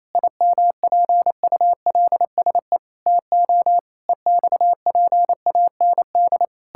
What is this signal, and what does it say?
Morse code, heard as a keyed tone, about 27 wpm, 700 Hz. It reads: IMPULSE TO EXPAND
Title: Pangram Challenge v2 - 28wpm